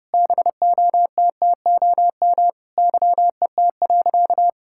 Morse code, heard as a keyed tone, about 30 wpm, 700 Hz. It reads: BOTTOM YET.
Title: Sentences from Top 1000 Words - Review Part 6 - 30wpm